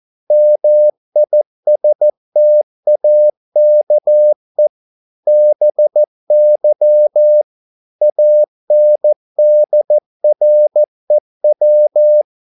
Morse code, heard as a keyed tone, about 14 wpm, 600 Hz. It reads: MISTAKE BY ANDREW
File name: mp3 files / a dogs mistake - 14wpm